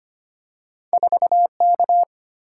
Morse code, heard as a keyed tone, about 25 wpm, 700 Hz. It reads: 4K
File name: Two Letter-Number Combinations - 25wpm